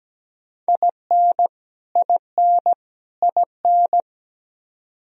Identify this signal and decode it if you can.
Morse code, heard as a keyed tone, about 17 wpm, 700 Hz. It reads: IN IN IN
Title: Top 100 Words Repeated 17wpm